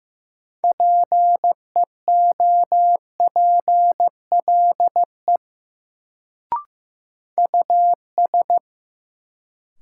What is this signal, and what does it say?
Morse code, heard as a keyed tone, about 15 wpm, 700 Hz. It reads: PEOPLE US
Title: Sentences from Top 200 Words - New Word Drill - 15wpm